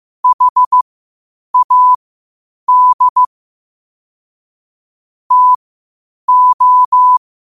Morse code, heard as a keyed tone, about 15 wpm, spaced as for 8 wpm, 1000 Hz. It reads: HAD TO
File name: qotdfiles / 8wpm